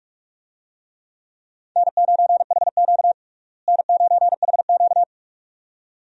Morse code, heard as a keyed tone, about 46 wpm, 700 Hz. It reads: N9HQ N9HQ
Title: USA Call Signs - 4 Characters - Repeated 2x - 45wpm